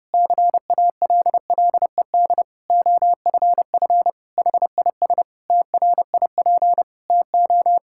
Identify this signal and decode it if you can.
Morse code, heard as a keyed tone, about 30 wpm, 700 Hz. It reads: CALLED OFF HIS TRIP TO
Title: Pangram Challenge v2 - 30wpm